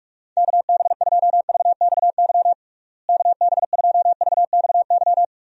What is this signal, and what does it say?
Morse code, heard as a keyed tone, about 41 wpm, 700 Hz. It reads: KB2VXY KB2VXY
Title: USA Call Signs - 6 Characters - Repeated 3x - 45wpm